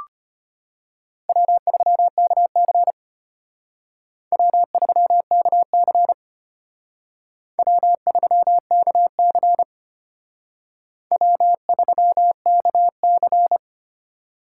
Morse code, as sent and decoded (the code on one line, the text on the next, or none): .-- ...-- -.- -.-. / .-- ...-- -.- -.-. / .-- ...-- -.- -.-. / .-- ...-- -.- -.-.
W3KC W3KC W3KC W3KC